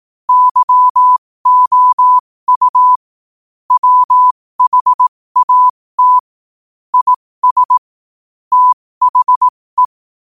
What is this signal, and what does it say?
Morse code, heard as a keyed tone, about 18 wpm, spaced as for 15 wpm, 1000 Hz. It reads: YOU WHAT IS THE
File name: qotdfiles / 15wpm